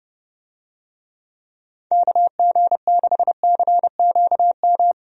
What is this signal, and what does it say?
Morse code, heard as a keyed tone, about 31 wpm, 700 Hz. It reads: KG6CQM
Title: USA Call Signs - 6 Characters - 30wpm